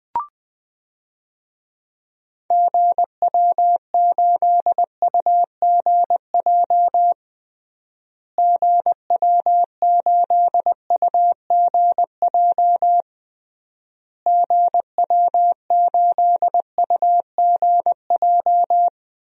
Morse code, heard as a keyed tone, about 20 wpm, 700 Hz. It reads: GW8UGJ GW8UGJ GW8UGJ
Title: DXCC-100 Call Signs - 6 Characters - Repeated 3x - 20wpm